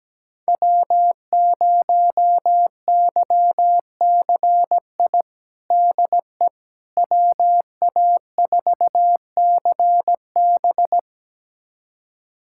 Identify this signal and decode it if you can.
Morse code, heard as a keyed tone, about 17 wpm, 700 Hz. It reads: W0YCI DE WA4CB
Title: USA Callsign DE Callsign - 5 Characters - 17wpm